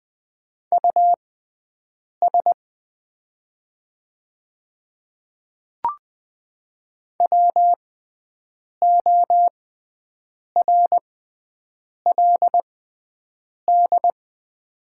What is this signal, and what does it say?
Morse code, heard as a keyed tone, about 20 wpm, spaced as for 7 wpm, 700 Hz. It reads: US WORLD
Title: 6x Character Spacing - Top 200 Words - 20wpm